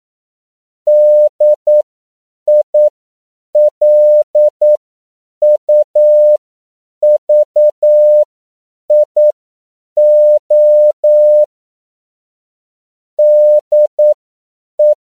Morse code, heard as a keyed tone, about 9 wpm, spaced as for 7 wpm, 600 Hz. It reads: DILUVIO DE